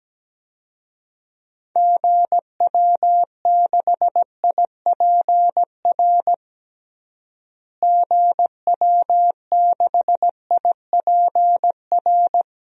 Morse code, heard as a keyed tone, about 17 wpm, 700 Hz. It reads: GW6IPR GW6IPR
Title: DXCC-100 Call Signs - 6 Characters - Repeated 3x - 17wpm